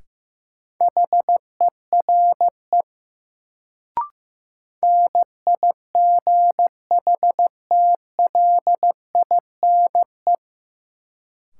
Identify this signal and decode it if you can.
Morse code, heard as a keyed tone, about 15 wpm, 700 Hz. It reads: HERE NIGHTLINE
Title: Sets of 5 Words - New Word Drill - 15wpm